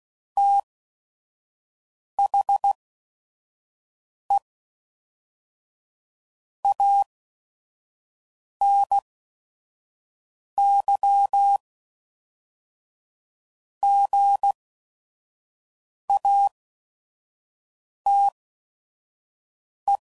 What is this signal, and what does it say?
Morse code, heard as a keyed tone, about 16 wpm, spaced as for 5 wpm, 800 Hz. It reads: THE ANY GATE
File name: CW Code Groups @5WPM / Technical Subjects / OR Gate